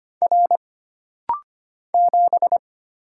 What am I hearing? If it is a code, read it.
Morse code, heard as a keyed tone, about 25 wpm, 700 Hz. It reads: R 7
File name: Single Character (Letter:Number:Punctuation:Prosign) - Rapid-Fire - 25wpm